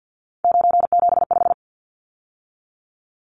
Morse code, heard as a keyed tone, about 50 wpm, 700 Hz. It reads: 975
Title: Numbers - 3-Digit 50wpm